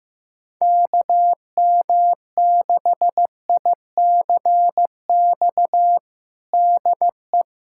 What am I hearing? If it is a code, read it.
Morse code, heard as a keyed tone, about 16 wpm, 700 Hz. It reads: KM6ICX DE
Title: USA Callsign DE Callsign - 6 Characters - 15wpm